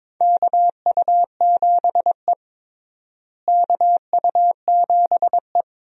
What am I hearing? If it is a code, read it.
Morse code, heard as a keyed tone, about 22 wpm, 700 Hz. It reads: KU7E KU7E
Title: USA Call Signs - 4 Characters - Repeated 2x - 22wpm